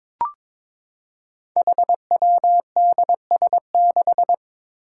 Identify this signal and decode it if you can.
Morse code, heard as a keyed tone, about 22 wpm, 700 Hz. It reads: HWDS6
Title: Five Challenging Letter-Number Combinations 22wpm